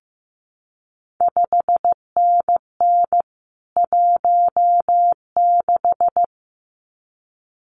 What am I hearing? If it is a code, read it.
Morse code, heard as a keyed tone, about 15 wpm, 700 Hz. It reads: 5NN 16
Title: CQ WW DX Contest Exchange - 15wpm